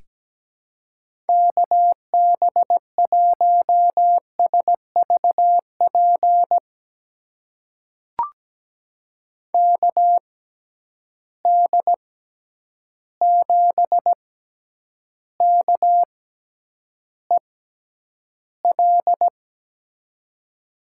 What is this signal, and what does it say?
Morse code, heard as a keyed tone, about 17 wpm, spaced as for 9 wpm, 700 Hz. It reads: KB1SVP KD7KEL
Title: USA Call Signs - 6x Character Spacing - 17wpm